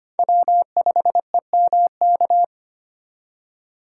Morse code, heard as a keyed tone, about 25 wpm, 700 Hz. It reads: W5EMK
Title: USA Call Signs - 5 Characters - 2x Character Spacing - 25wpm